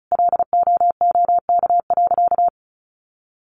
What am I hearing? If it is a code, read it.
Morse code, heard as a keyed tone, about 34 wpm, 700 Hz. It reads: LOOK.
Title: A Sound Of Thunder 35wpm